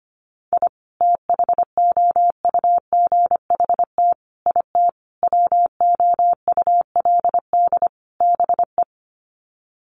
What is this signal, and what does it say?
Morse code, heard as a keyed tone, about 25 wpm, 700 Hz. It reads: I THOUGHT IT WOULD BE
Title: Sets of 5 Words Encore 25wpm